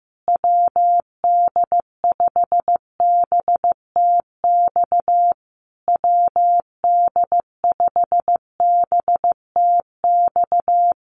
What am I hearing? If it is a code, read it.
Morse code, heard as a keyed tone, about 15 wpm, 700 Hz. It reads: WD5BTX WD5BTX
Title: Common QSO Phrases Encore 15wpm